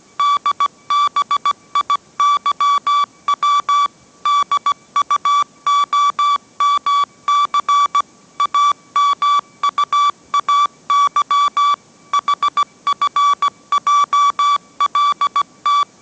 Morse code, heard as a keyed tone, about 17 wpm, 1150 Hz. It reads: DBIYW DUOMC AMUAY HFJLT